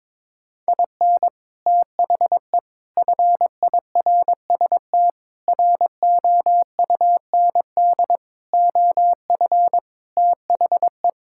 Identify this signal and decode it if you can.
Morse code, heard as a keyed tone, about 22 wpm, 700 Hz. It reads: IN THE FIRST ROUND OF THE
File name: practice / 22wpm / Sets of 6 Words - 22wpm